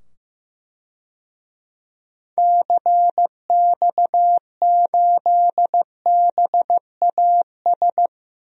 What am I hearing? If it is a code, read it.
Morse code, heard as a keyed tone, about 15 wpm, 700 Hz. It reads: CX8BAS
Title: DXCC-100 Call Signs - 6 Characters - 15wpm